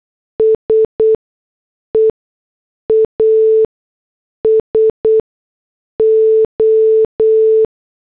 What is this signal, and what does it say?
Morse code, heard as a keyed tone, about 8 wpm, spaced as for 6 wpm, 440 Hz. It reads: SEASO